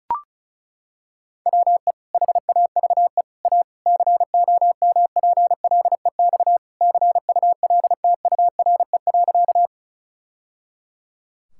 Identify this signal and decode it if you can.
Morse code, heard as a keyed tone, about 36 wpm, 700 Hz. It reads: WE HAVE A COMPLEX CULTURE.
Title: Sentences from Top 1000 Words - Review Part 6 - 35wpm